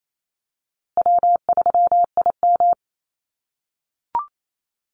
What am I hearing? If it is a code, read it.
Morse code, heard as a keyed tone, about 28 wpm, 700 Hz. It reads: W3IM
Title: USA Call Signs Repeated 2x 28wpm